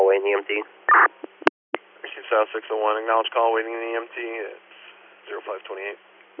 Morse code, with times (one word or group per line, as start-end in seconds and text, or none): none